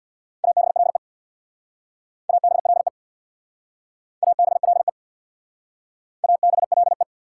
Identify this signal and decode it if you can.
Morse code, heard as a keyed tone, about 62 wpm, 690 Hz. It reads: ABLE ABLE ABLE ABLE
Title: Speed-Racing Prefixes and Suffixes 50wpm